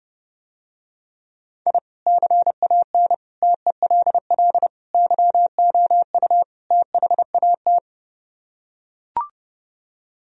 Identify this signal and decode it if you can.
Morse code, heard as a keyed tone, about 30 wpm, 700 Hz. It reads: I CAN TELL YOU THAT